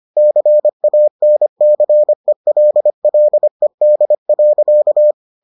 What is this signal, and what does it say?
Morse code, heard as a keyed tone, about 25 wpm, 600 Hz. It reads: CANCELLED.